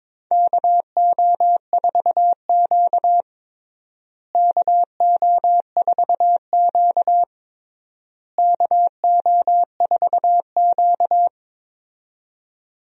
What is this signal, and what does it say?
Morse code, heard as a keyed tone, about 22 wpm, 700 Hz. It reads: KO4Q KO4Q KO4Q
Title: USA Call Signs - 4 Characters - Repeated 3x - 22wpm